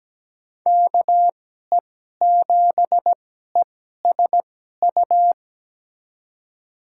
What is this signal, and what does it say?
Morse code, heard as a keyed tone, about 17 wpm, spaced as for 12 wpm, 700 Hz. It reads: KE7ESU
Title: USA Call Signs - 6 Characters - 2x Character Spacing - 17wpm